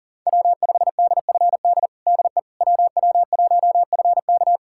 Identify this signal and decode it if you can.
Morse code, heard as a keyed tone, about 40 wpm, 700 Hz. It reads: W5DFD DE WW1FK
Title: USA Callsign DE Callsign - 5 Characters - 40wpm